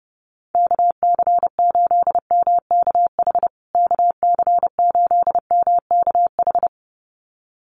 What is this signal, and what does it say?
Morse code, heard as a keyed tone, about 30 wpm, 700 Hz. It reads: KC8MKH KC8MKH
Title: USA Call Signs Repeated 2x 30wpm